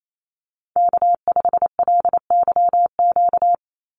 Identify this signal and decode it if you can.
Morse code, heard as a keyed tone, about 28 wpm, 700 Hz. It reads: K5LYQ
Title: USA Call Signs 28wpm